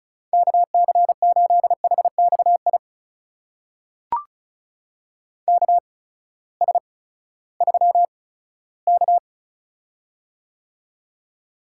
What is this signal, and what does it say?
Morse code, heard as a keyed tone, about 36 wpm, spaced as for 18 wpm, 700 Hz. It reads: KC8HXI KS3K
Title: USA Call Signs - 8x Character Spacing - 35wpm